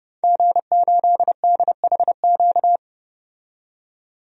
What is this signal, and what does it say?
Morse code, heard as a keyed tone, about 31 wpm, 700 Hz. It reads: G8DHQ
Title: DXCC-100 Call Signs - 5 Characters - 30wpm